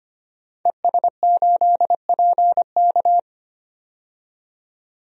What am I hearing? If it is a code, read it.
Morse code, heard as a keyed tone, about 25 wpm, 700 Hz. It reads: ES8PK
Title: DXCC-100 Call Signs - 5 Characters - 25wpm